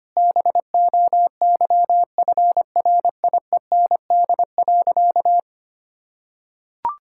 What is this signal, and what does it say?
Morse code, heard as a keyed tone, about 25 wpm, 700 Hz. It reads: BOYFRIEND.
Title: Sentences from Top 1000 Words - Review Part 5 - 25wpm